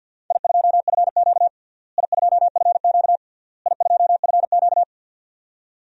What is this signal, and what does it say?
Morse code, heard as a keyed tone, about 52 wpm, 700 Hz. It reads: I2FQ I2FQ I2FQ
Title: DXCC-100 Call Signs - 4 Characters - Repeated 3x - 50wpm